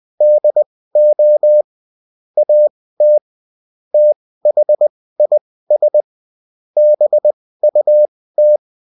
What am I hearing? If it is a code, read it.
Morse code, heard as a keyed tone, about 20 wpm, spaced as for 15 wpm, 600 Hz. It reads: DO AT THIS BUT
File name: Top-100-Words-15